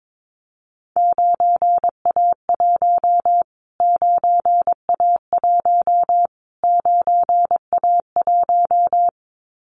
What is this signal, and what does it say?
Morse code, heard as a keyed tone, about 22 wpm, 700 Hz. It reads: 9A1 9A1 9A1
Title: Callsign Prefixes for Top 25 DXCC - Repeated 3x - 22wpm